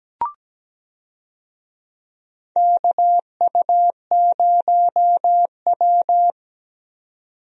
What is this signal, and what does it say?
Morse code, heard as a keyed tone, about 17 wpm, 700 Hz. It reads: KU0W
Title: USA Call Signs - 4 Characters - 17wpm